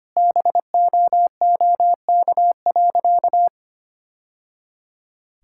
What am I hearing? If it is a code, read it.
Morse code, heard as a keyed tone, about 25 wpm, 700 Hz. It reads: BOOK.
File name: practice / 25wpm / Sentences with 2-Word Phrases - 25wpm